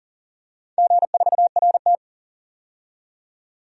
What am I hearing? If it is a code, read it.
Morse code, heard as a keyed tone, about 40 wpm, 700 Hz. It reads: G4RT
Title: DXCC-100 Call Signs - 4 Characters - 40wpm